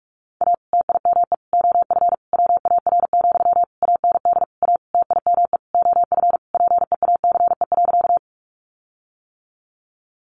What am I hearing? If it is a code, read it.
Morse code, heard as a keyed tone, about 44 wpm, 700 Hz. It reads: A TIME OF WAR, AND A TIME OF PEACE.
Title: Sentences from Top 1500 Words 45wpm